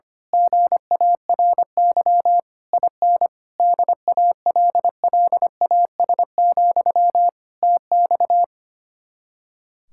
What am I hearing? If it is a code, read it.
Morse code, heard as a keyed tone, about 25 wpm, 700 Hz. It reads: GARY IN DALLAS, TX
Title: Speed-Racing Common QSO Phrases - 25wpm